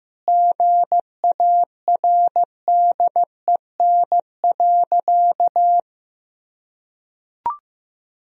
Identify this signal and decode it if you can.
Morse code, heard as a keyed tone, about 15 wpm, 700 Hz. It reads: GARDEN.
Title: Sentences from Top 1000 Words - Review Part 4 - 15wpm